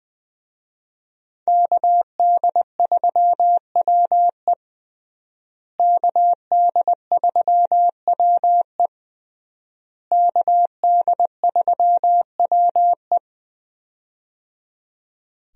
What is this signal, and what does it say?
Morse code, heard as a keyed tone, about 21 wpm, 700 Hz. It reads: KD3WE KD3WE KD3WE
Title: USA Call Signs - 5 Characters - Repeated 3x - 20wpm